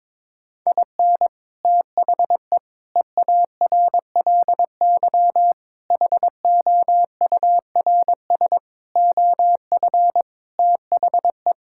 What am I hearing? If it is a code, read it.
Morse code, heard as a keyed tone, about 21 wpm, 700 Hz. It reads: IN THE EARLY HOURS OF THE
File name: Sentences with 7-Word Phrases - 22wpm